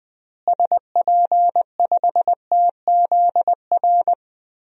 Morse code, heard as a keyed tone, about 21 wpm, 700 Hz. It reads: SP5TZR
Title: DXCC-100 Call Signs - 6 Characters - Repeated 3x - 20wpm